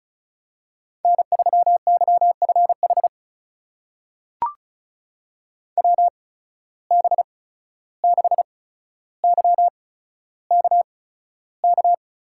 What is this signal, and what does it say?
Morse code, heard as a keyed tone, about 34 wpm, spaced as for 14 wpm, 700 Hz. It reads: N3YFH WB6YKK
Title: USA Call Signs - 8x Character Spacing - 35wpm